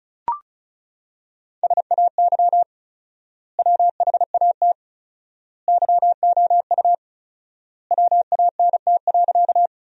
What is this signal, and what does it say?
Morse code, heard as a keyed tone, about 36 wpm, 700 Hz. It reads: SAY WHAT YOU WANT.
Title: 4x Word-Spacing - Sentences from Top 200 Words - 35wpm